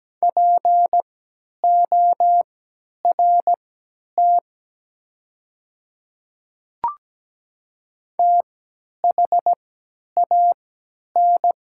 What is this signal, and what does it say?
Morse code, heard as a keyed tone, about 17 wpm, spaced as for 10 wpm, 700 Hz. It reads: PORT THAN
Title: 3x Character Spacing - Top 300 Words - 17wpm